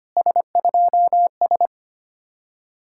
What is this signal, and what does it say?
Morse code, heard as a keyed tone, about 33 wpm, 700 Hz. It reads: S2S